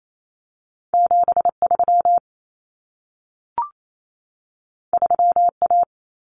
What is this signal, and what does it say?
Morse code, heard as a keyed tone, about 28 wpm, 700 Hz. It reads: 73 3A
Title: Common Cut Numbers - 2-Digit 28wpm